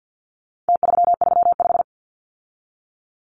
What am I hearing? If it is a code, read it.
Morse code, heard as a keyed tone, about 50 wpm, 700 Hz. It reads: T335
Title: Numbers - Contest Style - 4-Digit 50wpm